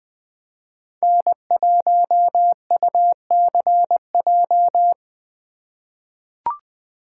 Morse code, heard as a keyed tone, about 20 wpm, 700 Hz. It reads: N1UCJ
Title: USA Call Signs - 5 Characters - 4x Character Spacing - 20wpm